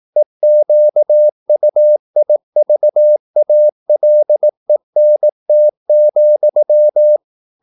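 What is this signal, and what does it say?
Morse code, heard as a keyed tone, about 18 wpm, 600 Hz. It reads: EQUIVALENT,
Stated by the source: Morse practice at 18 wpm, English text, from CW med Gnister.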